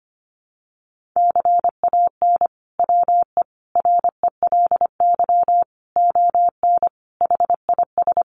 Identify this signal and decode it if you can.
Morse code, heard as a keyed tone, about 25 wpm, 700 Hz. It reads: CAN WE RELY ON HIS
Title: Sentences from Top 1200 Words 25wpm